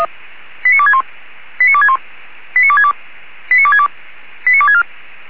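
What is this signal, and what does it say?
Five-tone selective call (ZVEI), heard as quick runs of five tones, tones of about 70 ms. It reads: 89261 89261 89262 89262 89264
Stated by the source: ZVEI five-tone selective call 89411, received on SDR.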